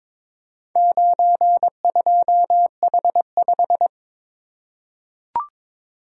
Morse code, heard as a keyed tone, about 23 wpm, 700 Hz. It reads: 92H5
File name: practice / 22wpm / Four Letter-Number Combinations - 22wpm